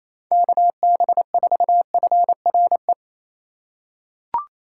Morse code, heard as a keyed tone, about 28 wpm, 700 Hz. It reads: KB4FRE
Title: USA Call Signs - 6 Characters - 4x Character Spacing - 28wpm